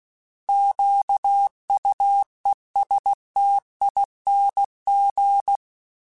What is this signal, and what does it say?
Morse code, heard as a keyed tone, about 16 wpm, 800 Hz. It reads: QUESTING